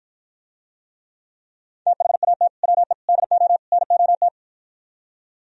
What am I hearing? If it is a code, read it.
Morse code, heard as a keyed tone, about 54 wpm, 700 Hz. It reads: THAT WE DO NOT